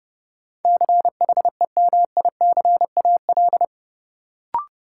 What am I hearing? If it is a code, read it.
Morse code, heard as a keyed tone, about 30 wpm, 700 Hz. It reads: CHEMICAL
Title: Sentences from Top 1200 Words - New Word Drill - 30wpm